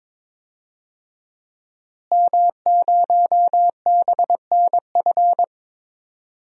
Morse code, heard as a keyed tone, about 22 wpm, 700 Hz. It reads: M0BNF
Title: DXCC-100 Call Signs - 5 Characters - Repeated 2x - 22wpm